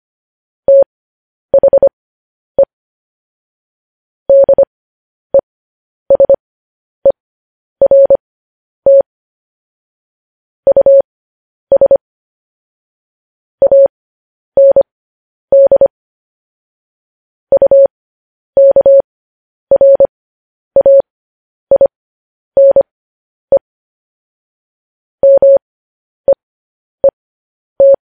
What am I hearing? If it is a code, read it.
Morse code, heard as a keyed tone, about 25 wpm, spaced as for 10 wpm, 550 Hz. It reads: THE DESERT US AND UKRAINE MEET
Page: news-headlines-in-morse-code-at-10-wpm